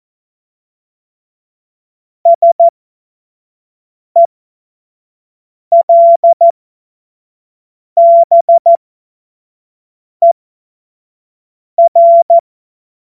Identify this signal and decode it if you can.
Morse code, heard as a keyed tone, about 14 wpm, spaced as for 5 wpm, 680 Hz. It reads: SELBER
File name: Witz-12-August-2025-25-F70